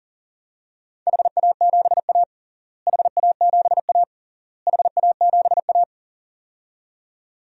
Morse code, heard as a keyed tone, about 40 wpm, 700 Hz. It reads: HA7A HA7A HA7A
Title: DXCC-100 Call Signs - 4 Characters - Repeated 3x - 40wpm